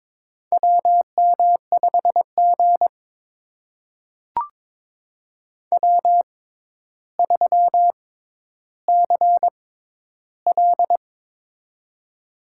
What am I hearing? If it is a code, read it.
Morse code, heard as a keyed tone, about 21 wpm, spaced as for 11 wpm, 700 Hz. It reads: WM5G W3CL
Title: USA Call Signs - 4 Characters - 6x Character Spacing - 22wpm